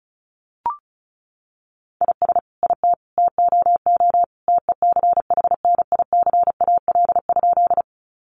Morse code, heard as a keyed tone, about 34 wpm, 700 Hz. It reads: IS IT TOO TECHNICAL?
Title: Sentences from Top 1100 Words 35wpm